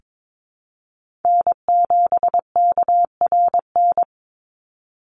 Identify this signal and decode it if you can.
Morse code, heard as a keyed tone, about 22 wpm, 700 Hz. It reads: N7KRN
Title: NA SOTA Chaser Callsigns 22wpm